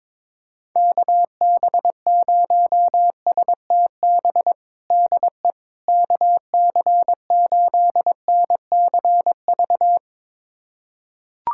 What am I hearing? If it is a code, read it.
Morse code, heard as a keyed tone, about 22 wpm, 700 Hz. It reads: KB0STB DE KC8NCV
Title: USA Callsign DE Callsign - 6 Characters - 22wpm